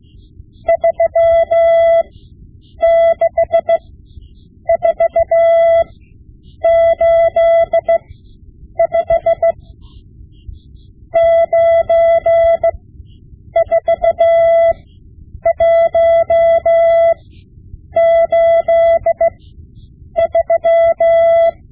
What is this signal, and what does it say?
Morse code, heard as a keyed tone, about 16 wpm, spaced as for 8 wpm, 670 Hz. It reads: 36485 94183
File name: Morse Lessons / Morse Lessons / Tape-5B